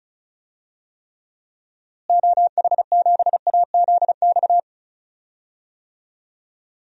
Morse code, heard as a keyed tone, about 36 wpm, 700 Hz. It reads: OH7AZX
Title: DXCC-100 Call Signs - 6 Characters - Repeated 2x - 35wpm